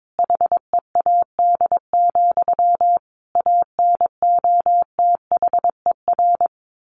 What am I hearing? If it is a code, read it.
Morse code, heard as a keyed tone, about 22 wpm, 700 Hz. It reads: HEAD, ANOTHER
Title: A Sound Of Thunder 22wpm